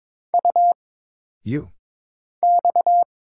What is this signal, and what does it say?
Morse code, heard as a keyed tone, about 22 wpm, 700 Hz. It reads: U X
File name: Single Character (Letter:Number:Punctuation:Prosign) - Mind-Melt - 22wpm